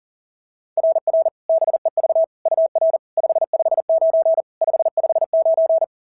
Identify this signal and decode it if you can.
Morse code, heard as a keyed tone, about 42 wpm, 690 Hz. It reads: RR BEV UR 559 559
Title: SOTA QSO Practice 40wpm